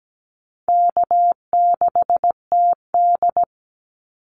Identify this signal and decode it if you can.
Morse code, heard as a keyed tone, about 17 wpm, 700 Hz. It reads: K6TD
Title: Easy Call Signs Repeated 2x - 17wpm